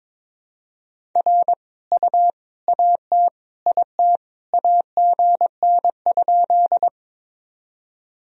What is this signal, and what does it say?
Morse code, heard as a keyed tone, about 22 wpm, 700 Hz. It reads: R U AT IT AGN?
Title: Sentences from Top 200 Words and Abbreviations 22wpm